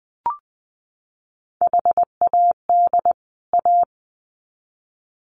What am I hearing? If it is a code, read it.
Morse code, heard as a keyed tone, about 20 wpm, 700 Hz. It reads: HAD A
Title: Sets of 2 Words 20wpm